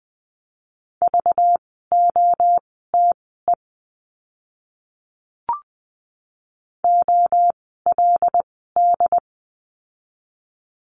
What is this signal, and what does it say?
Morse code, heard as a keyed tone, about 20 wpm, spaced as for 14 wpm, 700 Hz. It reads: VOTE OLD
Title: 192 Lesson 31f - Letter v - Words 20wpm